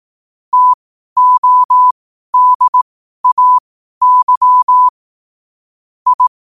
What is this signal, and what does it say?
Morse code, heard as a keyed tone, about 18 wpm, spaced as for 12 wpm, 1000 Hz. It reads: TODAY I
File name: qotdfiles / 12wpm